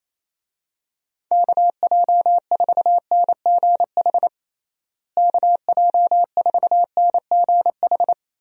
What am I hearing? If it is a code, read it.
Morse code, heard as a keyed tone, about 28 wpm, 700 Hz. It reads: KJ4NGH KJ4NGH
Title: USA Call Signs - 6 Characters - Repeated 2x - 28wpm